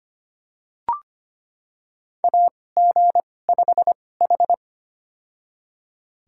Morse code, heard as a keyed tone, about 26 wpm, spaced as for 18 wpm, 700 Hz. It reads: AG5H
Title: USA Call Signs - 4 Characters - 2x Character Spacing - 25wpm